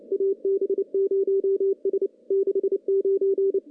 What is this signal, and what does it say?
Morse code, heard as a keyed tone, about 27 wpm, 390 Hz, with radio noise. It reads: AB0S 69
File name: contests / CW-SS-k5tr- -0700